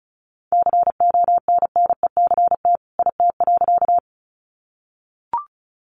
Morse code, heard as a keyed tone, about 34 wpm, 700 Hz. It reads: CONNECT IT.
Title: Sentences from Top 1500 Words 35wpm